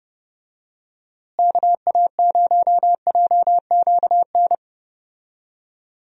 Morse code, heard as a keyed tone, about 30 wpm, 700 Hz. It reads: KA0JQN
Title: USA Call Signs - 6 Characters - Repeated 3x - 30wpm